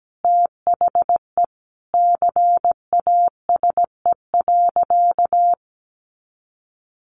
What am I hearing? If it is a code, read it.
Morse code, heard as a keyed tone, about 17 wpm, 700 Hz. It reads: THE CASE.
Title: Sentences from Top 1400 Words 17wpm